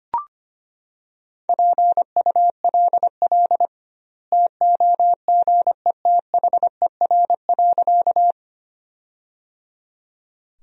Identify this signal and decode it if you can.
Morse code, heard as a keyed tone, about 25 wpm, 700 Hz. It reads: PULL TOGETHER.
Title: 2x Word-Spacing - Sentences from Top 400 Words - 25wpm